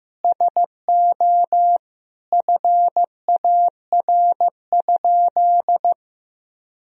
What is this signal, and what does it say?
Morse code, heard as a keyed tone, about 15 wpm, 700 Hz. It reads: SO FAR?
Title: Sentences from Top 300 Words and Abbreviations 15wpm